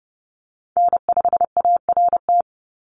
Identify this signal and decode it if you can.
Morse code, heard as a keyed tone, about 30 wpm, 700 Hz. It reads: N5ART
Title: USA Call Signs 30wpm